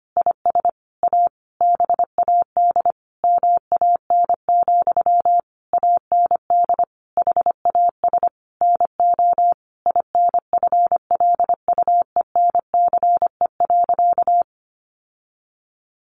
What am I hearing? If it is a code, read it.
Morse code, heard as a keyed tone, about 25 wpm, 700 Hz. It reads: IS A BAD MAN, AND HAS NO INFLUENCE.
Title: Sentences from Top 1100 Words 25wpm